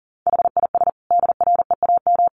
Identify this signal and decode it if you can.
Morse code, heard as a keyed tone, about 37 wpm, 700 Hz. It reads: HIS DREAM
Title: Sentences from Top 1500 Words 40wpm